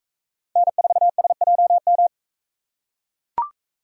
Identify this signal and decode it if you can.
Morse code, heard as a keyed tone, about 42 wpm, 700 Hz. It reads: N4SJM